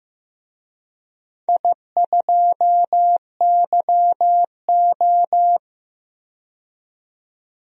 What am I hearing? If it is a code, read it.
Morse code, heard as a keyed tone, about 15 wpm, 700 Hz. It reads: I2YO
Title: DXCC-100 Call Signs - 4 Characters - 15wpm